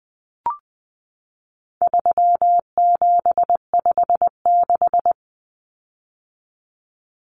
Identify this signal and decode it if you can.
Morse code, heard as a keyed tone, about 20 wpm, 700 Hz. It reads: 3756
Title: Common Cut Numbers - 4-Digit 20wpm